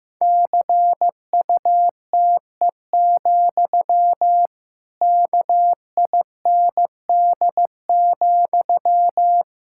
Morse code, heard as a keyed tone, about 15 wpm, 700 Hz. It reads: CUTE, KIND,
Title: Pangram Challenge v2 - 15wpm